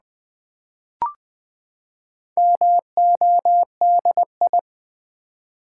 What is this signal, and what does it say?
Morse code, heard as a keyed tone, about 20 wpm, 700 Hz. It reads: MODI
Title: Build A Word - 4 Letters At A Time - 12-Letter Words - 20wpm